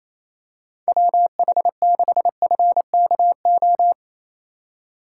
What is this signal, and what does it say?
Morse code, heard as a keyed tone, about 28 wpm, 700 Hz. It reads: WH6FKO
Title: USA Call Signs - 6 Characters - 2x Character Spacing - 28wpm